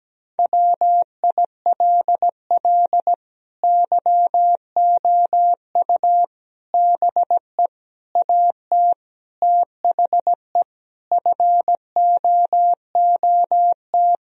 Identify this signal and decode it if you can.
Morse code, heard as a keyed tone, about 17 wpm, 700 Hz. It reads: WILL YOU BE AT THE FOOT
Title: Sentences from Top 1000 Words - Review Part 3 - 17wpm